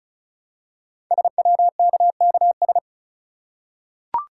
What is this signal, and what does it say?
Morse code, heard as a keyed tone, about 36 wpm, 700 Hz. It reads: SWKKS
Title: Code Groups - 5 Letters - 35wpm